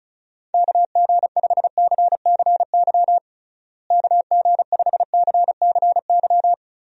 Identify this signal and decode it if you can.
Morse code, heard as a keyed tone, about 36 wpm, 700 Hz. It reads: KG5CCY KG5CCY
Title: USA Call Signs - 6 Characters - Repeated 2x - 35wpm